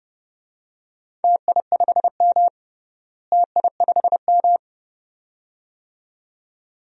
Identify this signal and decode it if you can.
Morse code, heard as a keyed tone, about 31 wpm, 700 Hz. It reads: TI5M TI5M
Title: DXCC-100 Call Signs - 4 Characters - Repeated 2x - 30wpm